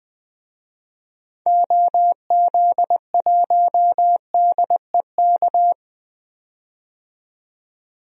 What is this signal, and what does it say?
Morse code, heard as a keyed tone, about 20 wpm, 700 Hz. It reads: OZ1DEK
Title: DXCC-100 Call Signs - 6 Characters - 20wpm